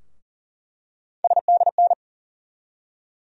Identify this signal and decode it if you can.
Morse code, heard as a keyed tone, about 40 wpm, 700 Hz. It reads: SDN
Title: USA Call Sign Postfixes - 3 Letters 40wpm